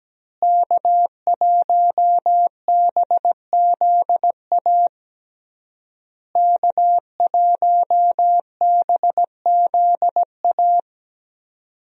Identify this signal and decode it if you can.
Morse code, heard as a keyed tone, about 17 wpm, 700 Hz. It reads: K1BZA K1BZA
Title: USA Call Signs - 5 Characters - Repeated 3x - 17wpm